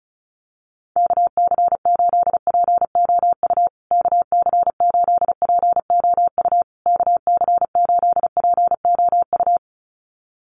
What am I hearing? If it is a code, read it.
Morse code, heard as a keyed tone, about 34 wpm, 700 Hz. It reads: KC8POU KC8POU KC8POU
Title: USA Call Signs Repeated 3x 35wpm